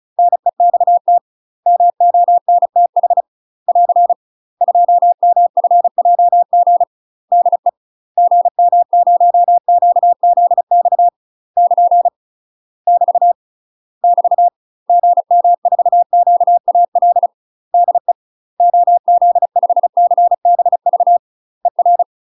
Morse code, heard as a keyed tone, about 36 wpm, 700 Hz. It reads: NEXT MONTH <AR> 2MFJG DE GM0QZX <KN> <BT> <BT> GM4QAL DE OZ5CBV ER
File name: UK MASTER QSOs TRAINING / 35WPM / UK MASTER QSOs 0011